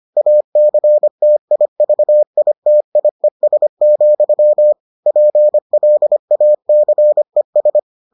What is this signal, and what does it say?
Morse code, heard as a keyed tone, about 25 wpm, 600 Hz. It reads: ACTIVITIES, PLACES